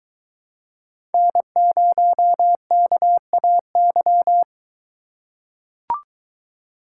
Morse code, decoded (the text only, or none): N0KAY